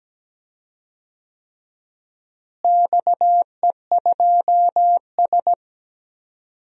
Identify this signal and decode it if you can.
Morse code, heard as a keyed tone, about 17 wpm, 700 Hz. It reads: XE2S
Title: DXCC-100 Call Signs - 4 Characters - Repeated 3x - 17wpm